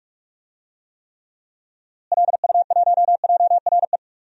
Morse code, heard as a keyed tone, about 46 wpm, 700 Hz. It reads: LU1JRE
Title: DXCC-100 Call Signs - 6 Characters - Repeated 3x - 45wpm